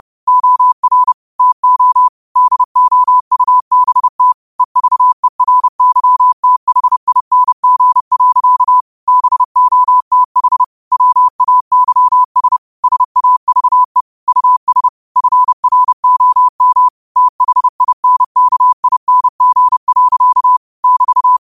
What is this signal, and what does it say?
Morse code, heard as a keyed tone, about 30 wpm, 1000 Hz. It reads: OR TO DOUBT EVERYTHING. BOTH WAYS SAVE US FROM THINKING. <BT>
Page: New quotes every day in morse code at 30 Words per minute.